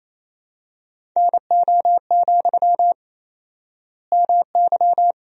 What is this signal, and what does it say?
Morse code, heard as a keyed tone, about 28 wpm, 700 Hz. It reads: NO, MY
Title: 4x Word-Spacing - Sentences from Top 400 Words - 28wpm